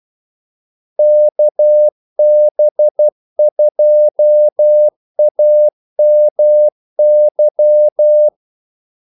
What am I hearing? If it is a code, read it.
Morse code, heard as a keyed tone, about 12 wpm, 600 Hz. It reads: KB2AMY
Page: 2 ekstra ordmellemrum mellem kaldesignalerne